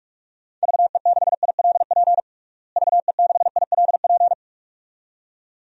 Morse code, heard as a keyed tone, about 47 wpm, 700 Hz. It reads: VE6ILP VE6ILP
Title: DXCC-100 Call Signs - 6 Characters - Repeated 2x - 45wpm